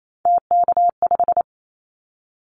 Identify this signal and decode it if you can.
Morse code, heard as a keyed tone, about 28 wpm, 700 Hz. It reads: TK5
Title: Callsign Prefixes for Top 100 DXCC - 28wpm